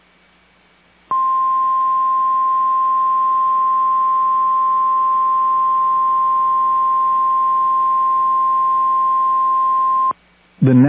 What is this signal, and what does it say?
Emergency Alert System (SAME) tones (the 1050 Hz attention tone). It is heard from 1.0 to 10.2 s.